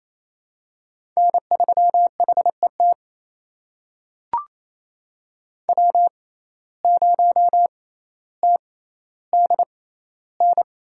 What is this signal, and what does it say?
Morse code, heard as a keyed tone, about 28 wpm, spaced as for 14 wpm, 700 Hz. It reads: N3HET W0TDN
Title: USA Call Signs - 5 Characters - 6x Character Spacing - 28wpm